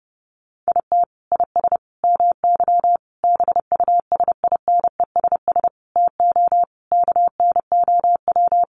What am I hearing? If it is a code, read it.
Morse code, heard as a keyed tone, about 30 wpm, 700 Hz. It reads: IT IS MY BUSINESS TO KNOW
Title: Sentences from Top 600 Words 30wpm